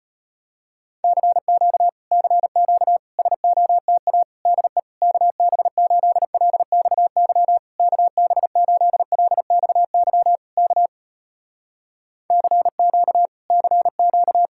Morse code, heard as a keyed tone, about 35 wpm, 700 Hz. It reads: CQ CQ SOTA DE KB8LXY KB8LXY K CQ CQ